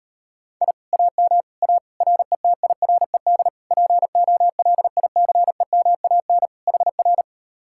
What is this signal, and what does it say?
Morse code, heard as a keyed tone, about 38 wpm, 700 Hz. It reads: I AM A RETIRED POLICEMAN HR